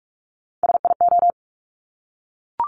Morse code, heard as a keyed tone, about 46 wpm, 690 Hz. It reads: SIO
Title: Top Three Letter Combinations in Words 45wpm